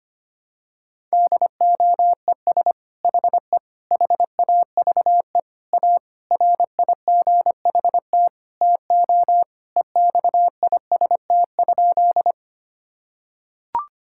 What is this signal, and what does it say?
Morse code, heard as a keyed tone, about 25 wpm, 700 Hz. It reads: DOES HE HAVE A RIGHT TO EXIST?
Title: Sentences from Top 1000 Words - Review Part 5 - 25wpm